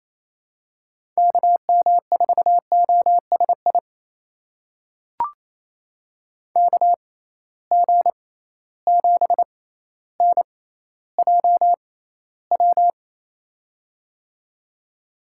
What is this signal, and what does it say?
Morse code, heard as a keyed tone, about 28 wpm, spaced as for 14 wpm, 700 Hz. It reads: KM4OSI KG7NJW
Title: USA Call Signs - 6 Characters - 6x Character Spacing - 28wpm